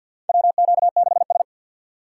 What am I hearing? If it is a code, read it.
Morse code, heard as a keyed tone, about 48 wpm, 700 Hz. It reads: WQ6S
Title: USA Call Signs - 4x Character Spacing - 50wpm